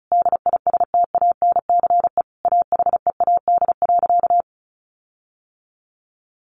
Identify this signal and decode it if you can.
Morse code, heard as a keyed tone, about 34 wpm, 700 Hz. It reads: DISTANCE AHEAD.
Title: Sentences from Top 1400 Words 35wpm